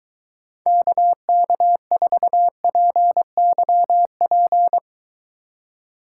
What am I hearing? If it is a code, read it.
Morse code, heard as a keyed tone, about 23 wpm, 700 Hz. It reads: KK4PYP